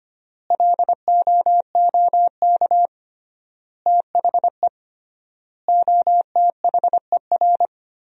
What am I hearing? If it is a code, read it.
Morse code, heard as a keyed tone, about 25 wpm, 700 Hz. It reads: LOOK THE OTHER
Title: 3x Word-Spacing - Sentences from Top 100 Words - 25wpm